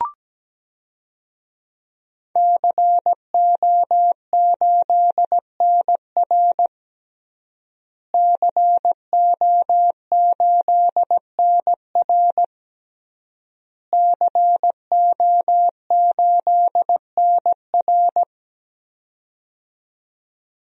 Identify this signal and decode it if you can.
Morse code, heard as a keyed tone, about 17 wpm, 700 Hz. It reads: CO8NR CO8NR CO8NR
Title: DXCC-100 Call Signs - 5 Characters - Repeated 3x - 17wpm